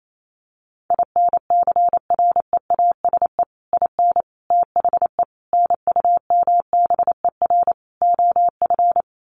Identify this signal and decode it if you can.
Morse code, heard as a keyed tone, about 28 wpm, 700 Hz. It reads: INCREASE IN THE NUMBER OF
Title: Sets of 5 Words 28wpm